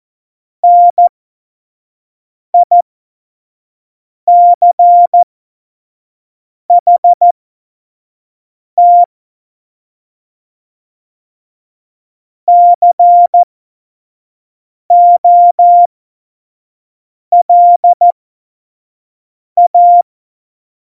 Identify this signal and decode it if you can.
Morse code, heard as a keyed tone, about 14 wpm, spaced as for 5 wpm, 700 Hz. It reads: NICHT COLA
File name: Morsekurs / Witze / Witz 01. Oktober 2019, 25 F70